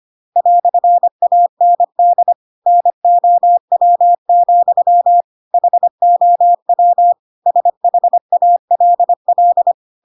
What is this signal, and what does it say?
Morse code, heard as a keyed tone, about 25 wpm, 700 Hz. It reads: "AND NOW, HOW SHALL
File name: Pinocchio / Pinocchio ch5 25 25 ik1hge